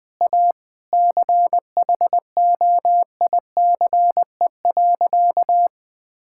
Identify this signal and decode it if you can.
Morse code, heard as a keyed tone, about 20 wpm, 700 Hz. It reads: A CHOICE.
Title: Sentences with 4-Word Phrases - 20wpm